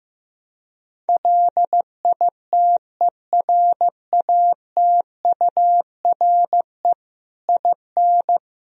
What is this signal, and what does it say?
Morse code, heard as a keyed tone, about 16 wpm, 700 Hz. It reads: LITERATURE IN
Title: Sentences from Top 1000 Words - Review Part 3 - 15wpm